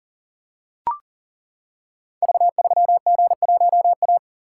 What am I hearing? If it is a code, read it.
Morse code, heard as a keyed tone, about 42 wpm, 690 Hz. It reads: V3G1A
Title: Code Groups - 5 Letter:Numbers - 40wpm